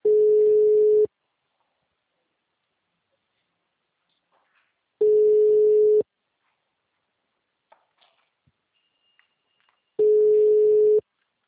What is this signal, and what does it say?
Ringback tone (Europe, 425 Hz).
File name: out- -2002- - - .63457